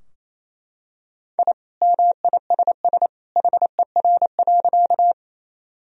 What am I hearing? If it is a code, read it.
Morse code, heard as a keyed tone, about 28 wpm, 700 Hz. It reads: I MISS HER.
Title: Sentences from Top 1000 Words - Review Part 3 - 28wpm